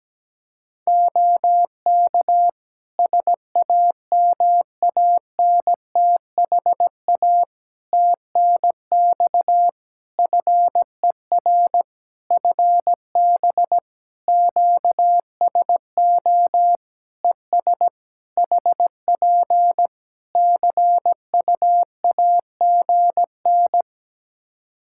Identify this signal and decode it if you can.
Morse code, heard as a keyed tone, about 17 wpm, 700 Hz. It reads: OK SAMANTHA TNX FER FB QSO ES HP CUAGN
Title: Practice QSO - Exchange 4 - Ending - 17wpm